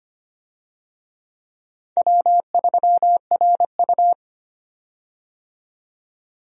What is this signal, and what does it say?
Morse code, heard as a keyed tone, about 25 wpm, 700 Hz. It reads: W3RU
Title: USA Call Signs - 4 Characters - 25wpm